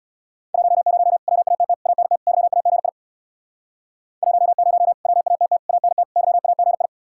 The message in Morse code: 10 WATT ANT YAGI 10 WATT ANT YAGI

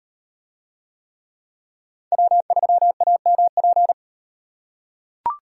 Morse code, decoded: W3AMP

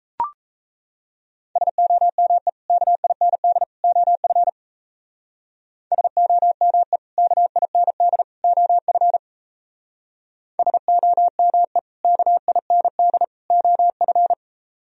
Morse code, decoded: SOME KIND OF SOME KIND OF SOME KIND OF

